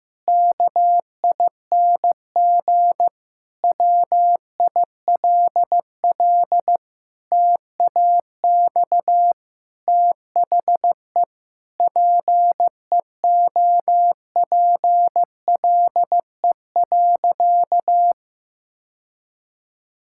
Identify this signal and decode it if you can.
Morse code, heard as a keyed tone, about 15 wpm, 700 Hz. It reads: KING WILL TAX THE PEOPLE.
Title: Sentences from Top 1000 Words - Review Part 6 - 15wpm